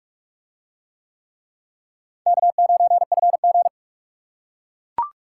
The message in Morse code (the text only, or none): K9FG